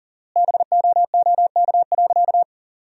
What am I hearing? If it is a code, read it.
Morse code, heard as a keyed tone, about 37 wpm, 700 Hz. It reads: BOOK.
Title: 4x Word-Spacing - Sentences from Top 500 Words - 40wpm